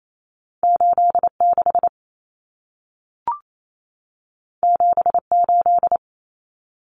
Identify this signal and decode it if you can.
Morse code, heard as a keyed tone, about 28 wpm, 700 Hz. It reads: 86 78
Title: Numbers - 2-Digit 28wpm